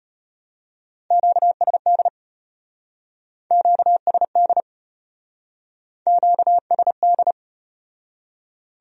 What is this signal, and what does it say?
Morse code, heard as a keyed tone, about 35 wpm, 700 Hz. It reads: QSD QSD QSD